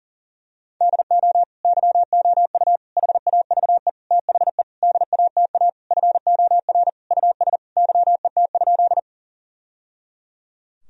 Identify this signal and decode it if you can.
Morse code, heard as a keyed tone, about 40 wpm, 700 Hz. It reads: DO YOU HAVE THE DATA FOR US YET?
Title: Sentences from Top 1000 Words - Review Part 3 - 40wpm